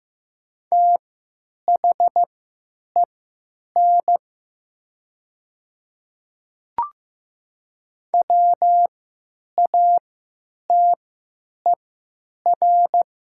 Morse code, heard as a keyed tone, about 15 wpm, spaced as for 9 wpm, 700 Hz. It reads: THEN WATER
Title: 3x Character Spacing - Top 100 Words - 15wpm